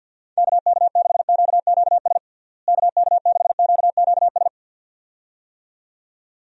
Morse code, heard as a keyed tone, about 49 wpm, 700 Hz. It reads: KK6QQS KK6QQS
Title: USA Call Signs - 6 Characters - Repeated 2x - 50wpm